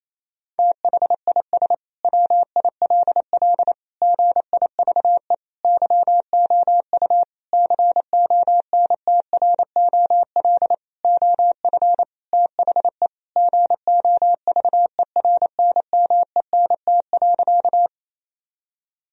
Morse code, heard as a keyed tone, about 27 wpm, 700 Hz. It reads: THIS WILL GIVE YOU CONTROL OF THE GOVERNMENT.
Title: Sentences from Top 1000 Words - Review Part 3 - 28wpm